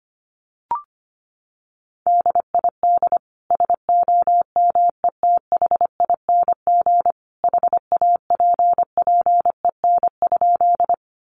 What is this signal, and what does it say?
Morse code, heard as a keyed tone, about 25 wpm, 700 Hz. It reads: DID SOMETHING HAPPEN?
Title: Sentences from Top 600 Words 25wpm